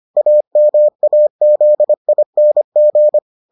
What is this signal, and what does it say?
Morse code, heard as a keyed tone, about 25 wpm, 600 Hz. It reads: AMAZING